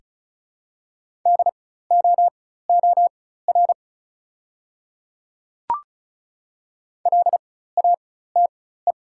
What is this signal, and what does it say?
Morse code, heard as a keyed tone, about 34 wpm, spaced as for 16 wpm, 690 Hz. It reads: DOOR LATE
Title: 4x Character Spacing - Top 300 Words - 35wpm